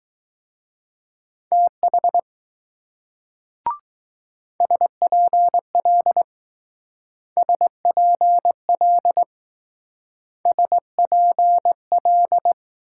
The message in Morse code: TH SPL SPL SPL

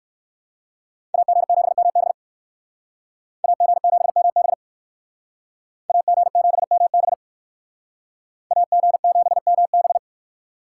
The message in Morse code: AG7MB AG7MB AG7MB AG7MB

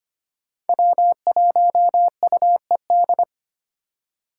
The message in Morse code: W1UED

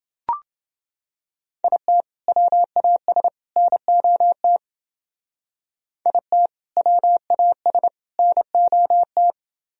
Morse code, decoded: IT WAS NOT IT WAS NOT

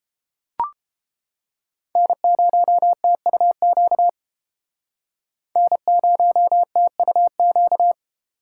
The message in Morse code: N0TUQ N0TUQ

